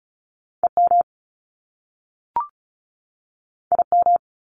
Morse code, heard as a keyed tone, about 36 wpm, 700 Hz. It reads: EM IM
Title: Top Two Letter Combinations in Words 35wpm